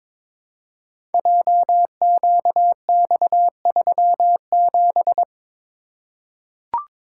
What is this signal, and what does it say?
Morse code, heard as a keyed tone, about 22 wpm, 700 Hz. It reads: JQX37
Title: Code Groups - 5 Letter:Numbers - 22wpm